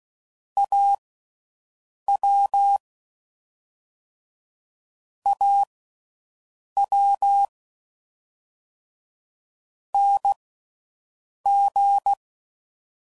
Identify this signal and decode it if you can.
Morse code, heard as a keyed tone, about 16 wpm, spaced as for 6 wpm, 800 Hz. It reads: AW AW NG